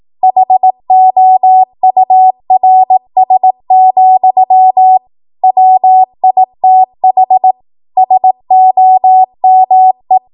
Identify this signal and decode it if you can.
Morse code, heard as a keyed tone, about 18 wpm, 750 Hz. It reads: HOURS, WITH SOME
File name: Archive / 18 WPM / 18WPM